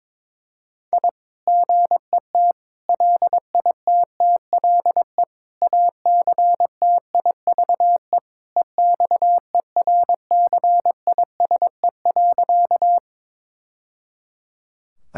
Morse code, heard as a keyed tone, about 22 wpm, 700 Hz. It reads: I GET LITTLE ACTIVE EXERCISE.
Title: Sentences from Top 1000 Words - Review Part 6 - 22wpm